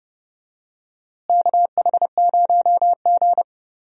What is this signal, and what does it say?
Morse code, heard as a keyed tone, about 30 wpm, 700 Hz. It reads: KH0G
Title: USA Call Signs - 4 Characters - Repeated 3x - 30wpm